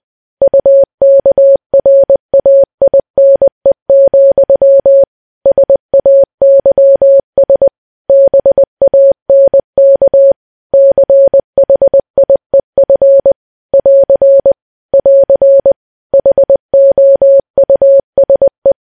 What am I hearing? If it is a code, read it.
Morse code, heard as a keyed tone, about 20 wpm, 550 Hz. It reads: UKRAINE, SAYS BANK CHIEF <AR> <AR> HOUSE
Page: news-headlines-in-morse-code-at-20-wpm